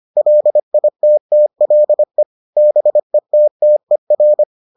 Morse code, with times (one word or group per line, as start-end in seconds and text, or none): 0.2-2.2 LITTLE
2.6-4.4 BETTER